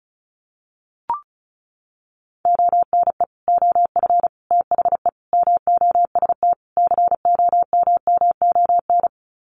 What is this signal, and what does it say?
Morse code, heard as a keyed tone, about 34 wpm, 700 Hz. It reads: ONE OF THE MOST COMMON
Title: Sets of 5 Words Encore2 35wpm